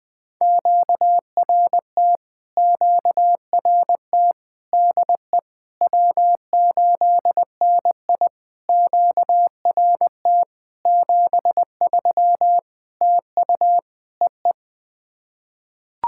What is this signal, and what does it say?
Morse code, heard as a keyed tone, about 20 wpm, 700 Hz. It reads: QRT QRT DE W8NI QRT 73 TU EE
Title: POTA QSO Practice - 20wpm